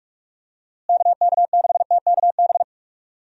Morse code, heard as a keyed tone, about 46 wpm, 700 Hz. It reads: KK6TKB
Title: USA Call Signs - 6 Characters - Repeated 3x - 45wpm